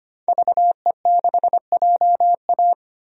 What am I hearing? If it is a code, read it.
Morse code, heard as a keyed tone, about 25 wpm, 700 Hz. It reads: VE6JA
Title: DXCC-100 Call Signs - 5 Characters - 25wpm